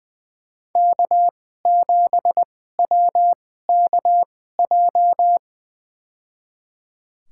Morse code, heard as a keyed tone, about 20 wpm, spaced as for 14 wpm, 700 Hz. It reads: K7WKJ
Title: USA Call Signs - 5 Characters - 2x Character Spacing - 20wpm